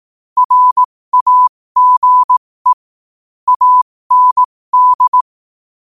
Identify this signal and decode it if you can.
Morse code, heard as a keyed tone, about 18 wpm, spaced as for 15 wpm, 1000 Hz. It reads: RAGE AND
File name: qotdfiles / 15wpm